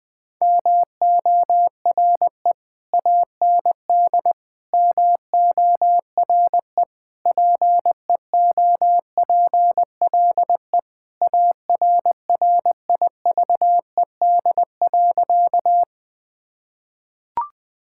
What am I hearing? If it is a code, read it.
Morse code, heard as a keyed tone, about 20 wpm, 700 Hz. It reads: MORE AND MORE PEOPLE ARRIVED.
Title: Sentences with 3-Word Phrases - 20wpm